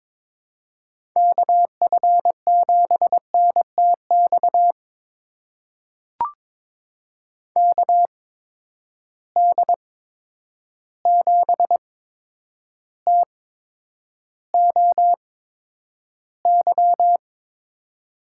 Morse code, heard as a keyed tone, about 22 wpm, spaced as for 9 wpm, 700 Hz. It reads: KF7NTX KD7TOY
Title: USA Call Signs - 6 Characters - 8x Character Spacing - 22wpm